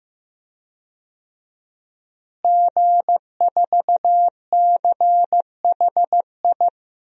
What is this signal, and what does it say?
Morse code, heard as a keyed tone, about 15 wpm, 700 Hz. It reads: G4CHI
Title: DXCC-100 Call Signs - 5 Characters - Repeated 2x - 15wpm